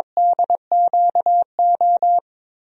Morse code, heard as a keyed tone, about 22 wpm, 700 Hz. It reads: DQO
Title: DXCC-100 Call Signs - 6 Characters - Repeated 2x - 22wpm